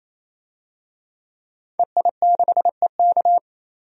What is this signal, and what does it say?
Morse code, heard as a keyed tone, about 27 wpm, 700 Hz. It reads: EI6EK
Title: DXCC-100 Call Signs - 5 Characters - 28wpm